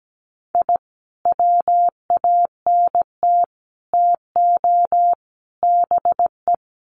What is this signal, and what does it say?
Morse code, heard as a keyed tone, about 17 wpm, 700 Hz. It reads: I WANT TO BE
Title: Sets of 4 Words 17wpm